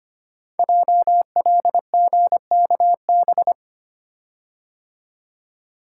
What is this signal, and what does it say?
Morse code, heard as a keyed tone, about 25 wpm, 700 Hz. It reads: JLGKB
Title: Code Groups - 5 Letters - 25wpm